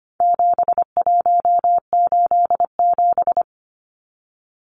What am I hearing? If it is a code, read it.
Morse code, heard as a keyed tone, about 25 wpm, 700 Hz. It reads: 7187
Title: Numbers - 4-Digit 25wpm